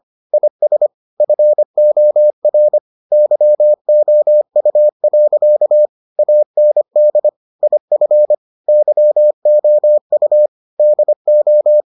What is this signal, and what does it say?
Morse code, heard as a keyed tone, about 25 wpm, 600 Hz. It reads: IS FOR YOU. AND IF YOU DO